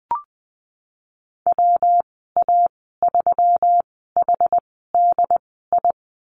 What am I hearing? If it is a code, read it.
Morse code, heard as a keyed tone, about 20 wpm, spaced as for 14 wpm, 700 Hz. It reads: WA3HDI
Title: 179 Lesson 29g - Slash - Call Signs 20wpm